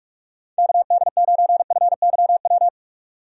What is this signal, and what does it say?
Morse code, heard as a keyed tone, about 46 wpm, 700 Hz. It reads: KD9FYW
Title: USA Call Signs - 6 Characters - 45wpm